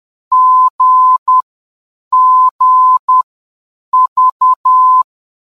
Morse code, heard as a keyed tone, about 10 wpm, spaced as for 7 wpm, 1020 Hz. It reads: GGV
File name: GGV-VOR DME